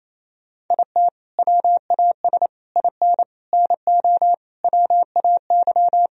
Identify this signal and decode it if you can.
Morse code, heard as a keyed tone, about 28 wpm, 700 Hz. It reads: IT WAS IN NO WAY
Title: Sentences from Top 1000 Words - Review Part 4 - 28wpm